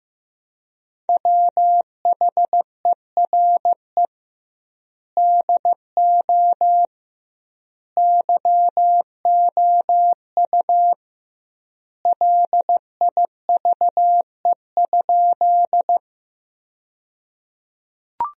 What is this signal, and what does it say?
Morse code, heard as a keyed tone, about 15 wpm, 700 Hz. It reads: WHERE DO YOU LIVE?
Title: 2x Word-Spacing - Sentences from Top 200 Words - 15wpm